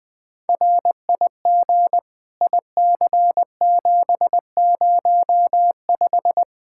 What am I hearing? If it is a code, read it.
Morse code, heard as a keyed tone, about 20 wpm, 700 Hz. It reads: RIG IC705
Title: Practice QSO - Exchange 2 - Info - 20wpm